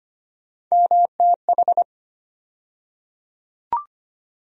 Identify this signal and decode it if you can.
Morse code, heard as a keyed tone, about 25 wpm, 700 Hz. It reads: MTH
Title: USA Call Sign Postfixes - 3 Letters 25wpm